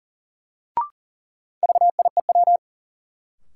Morse code, heard as a keyed tone, about 40 wpm, 700 Hz. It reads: VIEW
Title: Sentences from Top 900 Words - New Word Drill - 40wpm